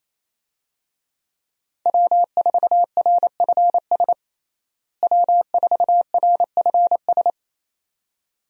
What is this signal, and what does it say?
Morse code, heard as a keyed tone, about 28 wpm, 700 Hz. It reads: W4RFS W4RFS
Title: USA Call Signs - 5 Characters - Repeated 2x - 28wpm